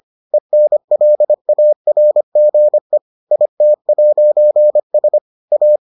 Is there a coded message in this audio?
Yes: Morse code from 0.3 to 5.8 s, heard as a keyed tone, about 25 wpm, 600 Hz. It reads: ENLARGE IT'S A